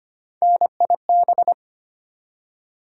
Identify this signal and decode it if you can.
Morse code, heard as a keyed tone, about 25 wpm, 700 Hz. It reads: NIB
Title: USA Call Sign Postfixes - 3 Letters 25wpm